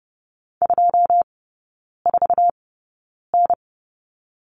Morse code, heard as a keyed tone, about 30 wpm, spaced as for 9 wpm, 700 Hz. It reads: 24N
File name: 081 Lesson 14e - Number 2 - Three Characters 30wpm